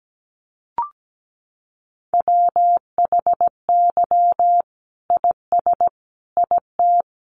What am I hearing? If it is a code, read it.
Morse code, heard as a keyed tone, about 17 wpm, 700 Hz. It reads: WHY IS IT
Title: Sentences from Top 900 Words 17wpm